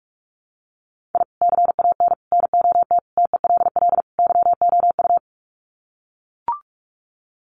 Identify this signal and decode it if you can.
Morse code, heard as a keyed tone, about 44 wpm, 700 Hz. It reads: I CAN NOT TELL YOU
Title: Sets of 5 Words 45wpm